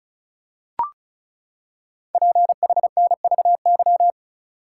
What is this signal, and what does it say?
Morse code, heard as a keyed tone, about 36 wpm, 700 Hz. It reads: PHNVY
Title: Code Groups - 5 Letters - 35wpm